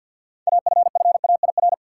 Morse code, heard as a keyed tone, about 52 wpm, 700 Hz. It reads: AFFAIR
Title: Six Letter Words - 50wpm